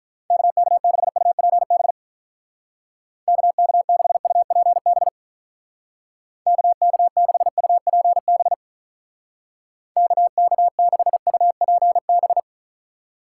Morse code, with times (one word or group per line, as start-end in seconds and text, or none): 0.3-1.9 KK6UPB
3.3-5.1 KK6UPB
6.5-8.5 KK6UPB
10.0-12.4 KK6UPB